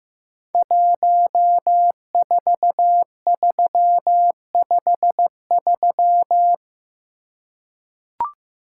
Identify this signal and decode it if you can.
Morse code, heard as a keyed tone, about 15 wpm, 700 Hz. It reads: 14353
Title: Code Groups - 5 Letter:Numbers - 15wpm